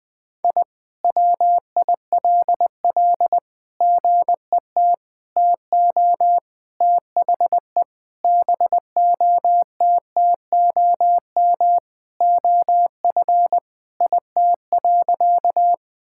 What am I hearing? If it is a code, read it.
Morse code, heard as a keyed tone, about 20 wpm, 700 Hz. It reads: I WILL GET TO THE BOTTOM OF IT.
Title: Sentences from Top 1000 Words - Review Part 6 - 20wpm